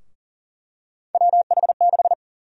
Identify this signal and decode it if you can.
Morse code, heard as a keyed tone, about 37 wpm, 700 Hz. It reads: WH6
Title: USA Call Sign Prefixes - Repeated 2x - 40wpm